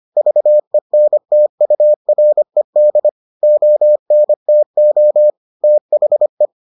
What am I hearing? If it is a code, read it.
Morse code, heard as a keyed tone, about 25 wpm, 600 Hz. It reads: VENTURED ONTO THE